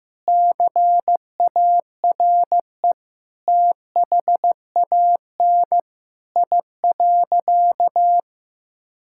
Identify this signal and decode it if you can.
Morse code, heard as a keyed tone, about 15 wpm, 700 Hz. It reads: CARE THAN I.
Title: Sentences from Top 1000 Words - Review Part 5 - 15wpm